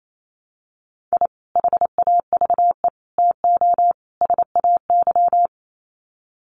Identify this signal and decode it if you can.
Morse code, heard as a keyed tone, about 28 wpm, 700 Hz. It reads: I HAVE TO SAY
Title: Sets of 4 Words 28wpm